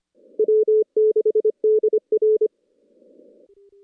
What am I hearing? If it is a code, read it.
Morse code, heard as a keyed tone, about 26 wpm, 410 Hz. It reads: W6DR
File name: contests / CW-SS-k5tr- -2300